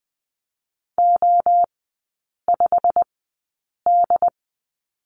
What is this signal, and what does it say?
Morse code, heard as a keyed tone, about 20 wpm, spaced as for 8 wpm, 690 Hz. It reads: O5D
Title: 087 Lesson 15e - Number 5 - Three Characters 20wpm